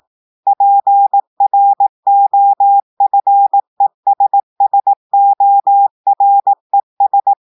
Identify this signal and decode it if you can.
Morse code, heard as a keyed tone, about 18 wpm, 800 Hz. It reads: PROFESSORES
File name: audios / portugais / portugais texte en clair 1 vitesse 18 mots minutes